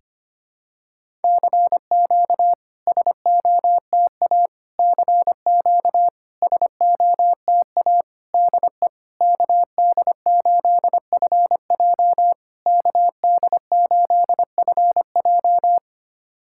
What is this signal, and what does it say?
Morse code, heard as a keyed tone, about 25 wpm, 700 Hz. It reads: CQ SOTA CQ SOTA DE KD8FJ KD8FJ
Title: SOTA QSO Practice 25wpm